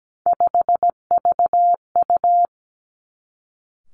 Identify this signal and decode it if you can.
Morse code, heard as a keyed tone, about 17 wpm, 700 Hz. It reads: 5VU
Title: Three Challenging Letter-Number Combinations 17wpm